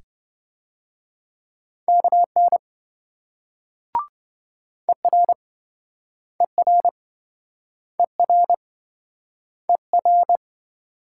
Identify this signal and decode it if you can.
Morse code, heard as a keyed tone, about 26 wpm, 700 Hz. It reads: KN ER ER ER ER